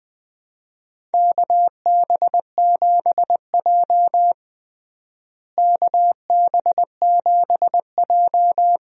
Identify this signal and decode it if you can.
Morse code, heard as a keyed tone, about 20 wpm, 700 Hz. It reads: KB7J KB7J
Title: USA Call Signs - 4 Characters - Repeated 3x - 20wpm